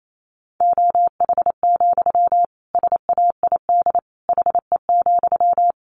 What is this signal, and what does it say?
Morse code, heard as a keyed tone, about 28 wpm, 700 Hz. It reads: OH, SAID HE,
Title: Sentences from Top 500 Words 28wpm